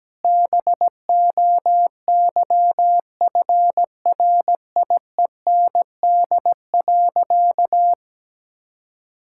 Morse code, heard as a keyed tone, about 17 wpm, 700 Hz. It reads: BOYFRIEND.
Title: Sentences from Top 1000 Words - Review Part 6 - 17wpm